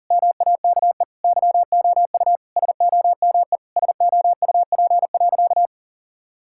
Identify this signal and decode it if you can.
Morse code, heard as a keyed tone, about 40 wpm, 700 Hz. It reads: MAKE YOU SOME SOUP.
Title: Sentences from Top 1000 Words - Review Part 6 - 40wpm